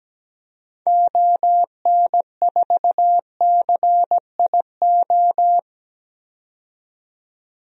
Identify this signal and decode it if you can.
Morse code, heard as a keyed tone, about 17 wpm, 700 Hz. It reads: ON4CIO
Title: DXCC-100 Call Signs - 6 Characters - 17wpm